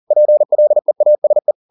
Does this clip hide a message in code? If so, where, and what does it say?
Yes: Morse code from 0.1 to 1.5 s, heard as a keyed tone, about 40 wpm, 600 Hz. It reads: PLEASE